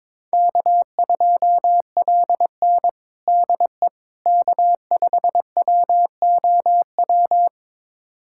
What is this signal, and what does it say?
Morse code, heard as a keyed tone, about 22 wpm, 700 Hz. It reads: K2LN DE K5WOW
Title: USA Callsign DE Callsign 22wpm